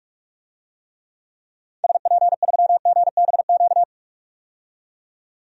Morse code, heard as a keyed tone, about 46 wpm, 700 Hz. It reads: SP3GBQ
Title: DXCC-100 Call Signs - 6 Characters - 45wpm